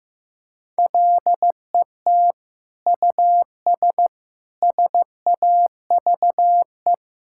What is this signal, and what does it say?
Morse code, heard as a keyed tone, about 15 wpm, 700 Hz. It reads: LET US SAVE
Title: Sentences from Top 1600 Words 15wpm